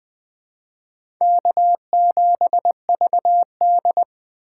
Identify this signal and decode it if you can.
Morse code, heard as a keyed tone, about 20 wpm, 700 Hz. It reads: K7VD
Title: USA Call Signs - 4 Characters - 3x Character Spacing - 20wpm